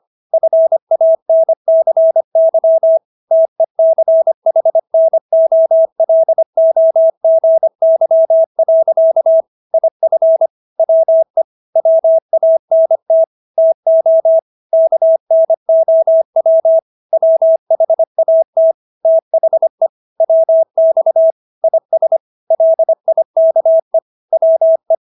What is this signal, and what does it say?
Morse code, heard as a keyed tone, about 25 wpm, 650 Hz. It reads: FANCY TECHNOLOGY. IF WE WANT TO KNOW WHAT THE WX IS LIKE WE